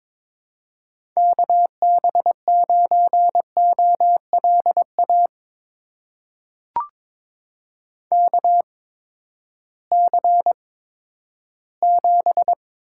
Morse code, heard as a keyed tone, about 22 wpm, spaced as for 12 wpm, 700 Hz. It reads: KB9OLA KC7
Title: USA Call Signs - 6 Characters - 8x Character Spacing - 22wpm